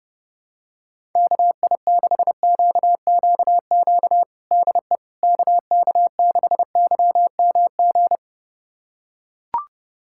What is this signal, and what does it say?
Morse code, heard as a keyed tone, about 30 wpm, 700 Hz. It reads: KI6QQQ DE KK6YMG
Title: USA Callsign DE Callsign 30wpm